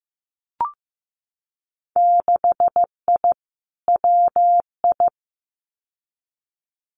Morse code, heard as a keyed tone, about 15 wpm, 700 Hz. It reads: 6I WI
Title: Winter Field Day Exchanges Encore 15wpm